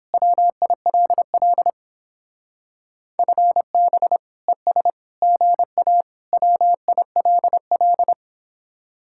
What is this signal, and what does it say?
Morse code, heard as a keyed tone, about 29 wpm, 700 Hz. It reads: WILL FB ES GA WILL